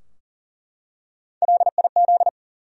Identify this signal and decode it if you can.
Morse code, heard as a keyed tone, about 37 wpm, 700 Hz. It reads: LIZ
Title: USA Common Names - Max 3 Letters - 40wpm